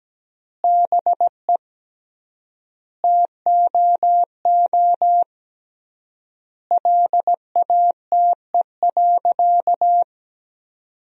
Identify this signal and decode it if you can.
Morse code, heard as a keyed tone, about 17 wpm, 700 Hz. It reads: BE TOO LATE.
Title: 3x Word-Spacing - Sentences from Top 300 Words - 17wpm